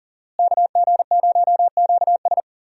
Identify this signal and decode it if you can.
Morse code, heard as a keyed tone, about 39 wpm, 700 Hz. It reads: KG0QS
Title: USA Call Signs - 5 Characters - Repeated 2x - 40wpm